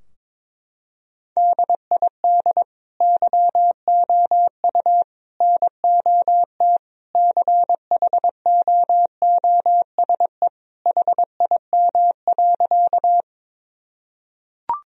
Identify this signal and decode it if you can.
Morse code, heard as a keyed tone, about 22 wpm, 700 Hz. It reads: DID YOU NOT CHOOSE HIM.
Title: Sentences from Top 1000 Words - Review Part 4 - 22wpm